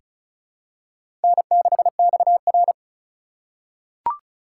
Morse code, heard as a keyed tone, about 34 wpm, 700 Hz. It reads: N6XR
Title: USA Call Signs - 4 Characters - 2x Character Spacing - 35wpm